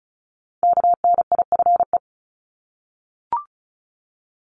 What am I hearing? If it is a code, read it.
Morse code, heard as a keyed tone, about 36 wpm, 700 Hz. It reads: KNIFE
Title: Multi-letter Phonemes Repeated 3x 35wpm